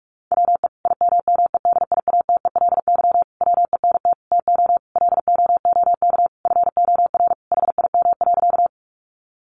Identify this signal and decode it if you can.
Morse code, heard as a keyed tone, about 44 wpm, 700 Hz. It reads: WE IMMEDIATELY WENT TO LOOK FOR HIM.
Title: Sentences from Top 1100 Words 45wpm